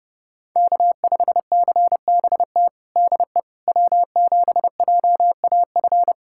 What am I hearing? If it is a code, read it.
Morse code, heard as a keyed tone, about 32 wpm, 700 Hz. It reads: K5CBT DE W7JAF
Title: USA Callsign DE Callsign - 5 Characters - 30wpm